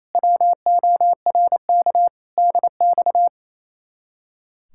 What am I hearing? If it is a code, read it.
Morse code, heard as a keyed tone, about 35 wpm, 700 Hz. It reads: WORK DX